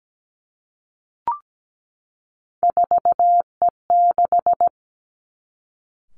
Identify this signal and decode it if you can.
Morse code, heard as a keyed tone, about 17 wpm, 700 Hz. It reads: 4E6
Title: Three Challenging Letter-Number Combinations 17wpm